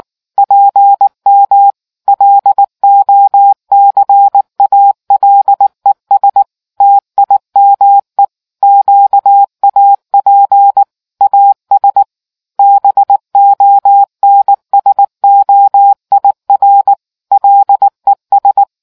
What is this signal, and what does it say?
Morse code, heard as a keyed tone, about 19 wpm, 800 Hz. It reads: PM LOCALES TIME QAP AS BONSOIR LES